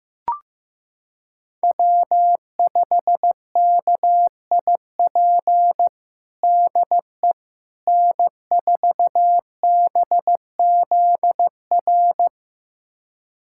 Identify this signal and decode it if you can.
Morse code, heard as a keyed tone, about 15 wpm, 700 Hz. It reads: W5KIP DE N4BZR
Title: USA Callsign DE Callsign - 5 Characters - 15wpm